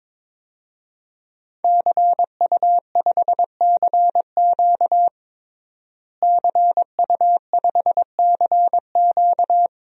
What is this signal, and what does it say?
Morse code, heard as a keyed tone, about 22 wpm, 700 Hz. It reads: CU5CQ CU5CQ
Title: DXCC-100 Call Signs - 5 Characters - Repeated 3x - 22wpm